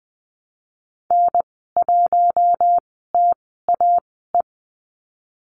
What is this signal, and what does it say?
Morse code, heard as a keyed tone, about 20 wpm, spaced as for 14 wpm, 700 Hz. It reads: N1TAE
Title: 062 Lesson 11g - Letter h - Call Signs 20wpm